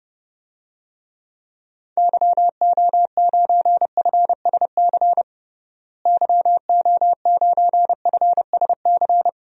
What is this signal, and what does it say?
Morse code, heard as a keyed tone, about 32 wpm, 700 Hz. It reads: YO9FSC YO9FSC
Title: DXCC-100 Call Signs - 6 Characters - Repeated 2x - 30wpm